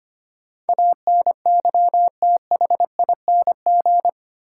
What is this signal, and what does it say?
Morse code, heard as a keyed tone, about 25 wpm, 700 Hz. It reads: ANYTHING
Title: Sets of 4 Words - New Word Drill - 25wpm